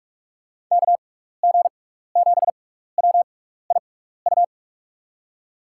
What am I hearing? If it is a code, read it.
Morse code, heard as a keyed tone, about 42 wpm, spaced as for 15 wpm, 700 Hz. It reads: KG7WIU
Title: USA Call Signs - 6 Characters - 6x Character Spacing - 45wpm